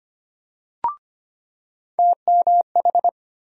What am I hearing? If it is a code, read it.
Morse code, heard as a keyed tone, about 25 wpm, 700 Hz. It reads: TMH
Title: USA Call Sign Postfixes - 3 Letters 25wpm